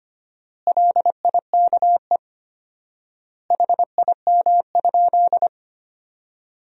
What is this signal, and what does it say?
Morse code, heard as a keyed tone, about 25 wpm, 700 Hz. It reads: LIKE HIM?
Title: 4x Word-Spacing - Sentences from Top 100 Words - 25wpm